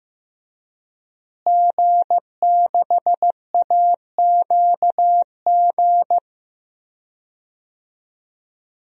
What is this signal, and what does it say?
Morse code, heard as a keyed tone, about 15 wpm, 700 Hz. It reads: G6AQG
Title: DXCC-100 Call Signs - 5 Characters - Repeated 3x - 15wpm